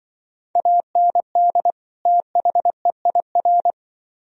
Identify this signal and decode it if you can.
Morse code, heard as a keyed tone, about 25 wpm, 700 Hz. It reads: AND THEIR